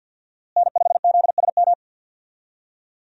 Morse code, heard as a keyed tone, about 45 wpm, 700 Hz. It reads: N5ZSM
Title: USA Call Signs - 5 Characters - Repeated 3x - 50wpm